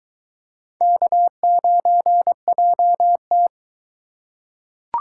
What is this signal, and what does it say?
Morse code, heard as a keyed tone, about 23 wpm, 700 Hz. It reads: K9JT